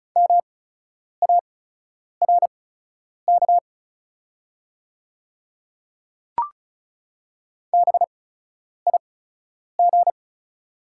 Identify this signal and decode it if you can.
Morse code, heard as a keyed tone, about 34 wpm, spaced as for 10 wpm, 690 Hz. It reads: MARK BIG
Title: 8x Character Spacing - Top 500 Words - 35wpm